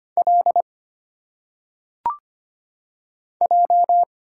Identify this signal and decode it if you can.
Morse code, heard as a keyed tone, about 25 wpm, 700 Hz. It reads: L J
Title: Single Letters - Phonetic Pronunciation 25wpm